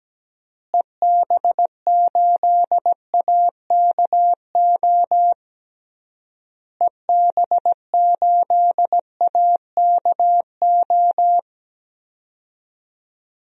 Morse code, heard as a keyed tone, about 17 wpm, 700 Hz. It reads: EB8AKO EB8AKO
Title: DXCC-100 Call Signs - 6 Characters - Repeated 3x - 17wpm